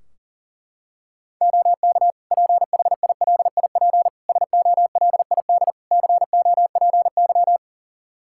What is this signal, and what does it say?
Morse code, heard as a keyed tone, about 40 wpm, 700 Hz. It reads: OK PHILIP SOLID COPY
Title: K1USN SST QSO Practice 40wpm